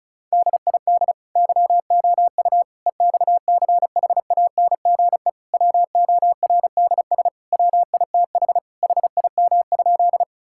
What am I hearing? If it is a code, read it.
Morse code, heard as a keyed tone, about 34 wpm, 700 Hz. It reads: DID YOU EXCHANGE WORDS WITH HIM?
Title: Sentences from Top 1000 Words - Review Part 6 - 35wpm